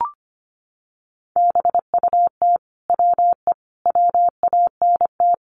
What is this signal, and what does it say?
Morse code, heard as a keyed tone, about 25 wpm, 700 Hz. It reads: BUT WE WANT
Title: Sentences from Top 700 Words 25wpm